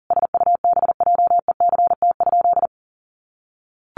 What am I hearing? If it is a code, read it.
Morse code, heard as a keyed tone, about 37 wpm, 700 Hz. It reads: SUBJECT?
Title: Sentences from Top 1200 Words 40wpm